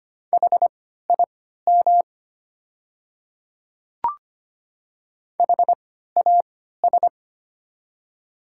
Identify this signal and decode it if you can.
Morse code, heard as a keyed tone, about 25 wpm, spaced as for 14 wpm, 700 Hz. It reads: HIM HAS
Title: 3x Character Spacing - Top 100 Words - 25wpm